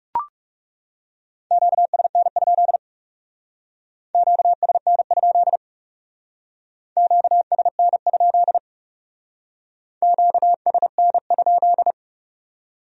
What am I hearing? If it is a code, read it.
Morse code, heard as a keyed tone, about 38 wpm, 700 Hz. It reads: QSN? QSN? QSN? QSN?